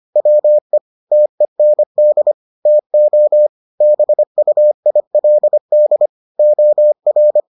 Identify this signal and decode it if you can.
Morse code, heard as a keyed tone, about 25 wpm, 600 Hz. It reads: WE TEND TO BUILD OR